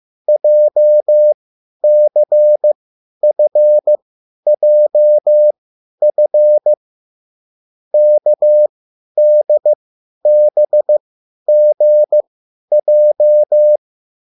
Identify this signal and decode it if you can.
Morse code, heard as a keyed tone, about 15 wpm, spaced as for 10 wpm, 600 Hz. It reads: JCFJF KDBGJ
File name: QSL / CW lesson